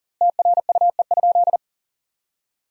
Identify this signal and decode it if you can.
Morse code, heard as a keyed tone, about 40 wpm, 700 Hz. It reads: TRUE?
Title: 2x Word-Spacing - Sentences from Top 400 Words - 40wpm